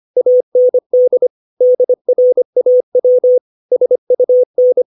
Morse code, heard as a keyed tone, about 25 wpm, 500 Hz. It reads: AND DRAW SUN
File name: 632 words / 632 Words 25WPM